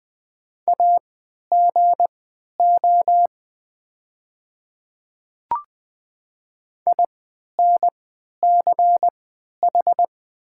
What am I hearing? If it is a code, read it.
Morse code, heard as a keyed tone, about 20 wpm, spaced as for 11 wpm, 700 Hz. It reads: AGO INCH
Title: 3x Character Spacing - Top 500 Words - 20wpm